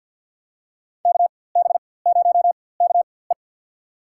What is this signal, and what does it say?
Morse code, heard as a keyed tone, about 53 wpm, spaced as for 24 wpm, 700 Hz. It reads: KB0KE
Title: USA Call Signs - 4x Character Spacing - 50wpm